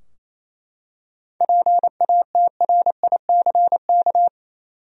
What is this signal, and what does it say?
Morse code, heard as a keyed tone, about 28 wpm, 700 Hz. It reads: PATRICK
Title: USA Common Names - 28wpm